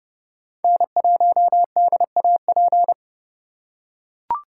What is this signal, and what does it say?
Morse code, heard as a keyed tone, about 30 wpm, 700 Hz. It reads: N1DAP
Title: USA Call Signs - 5 Characters - 3x Character Spacing - 30wpm